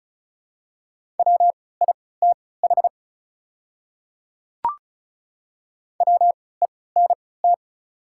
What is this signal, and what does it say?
Morse code, heard as a keyed tone, about 36 wpm, spaced as for 20 wpm, 700 Hz. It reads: WITH WENT
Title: 3x Character Spacing - Top 200 Words - 35wpm